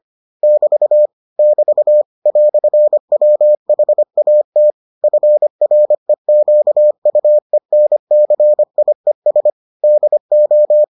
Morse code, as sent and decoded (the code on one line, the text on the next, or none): -...- / -...- / .-..-. .-- .... .- - / ..-. .-. --.- ..- . -. -.-. .. ... / -.. ---
<BT> <BT> "WHAT FREQUENCIES DO